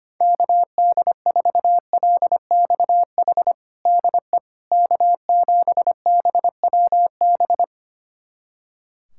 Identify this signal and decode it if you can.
Morse code, heard as a keyed tone, about 25 wpm, 700 Hz. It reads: KD4LXH DE K7BWB
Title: USA Callsign DE Callsign 25wpm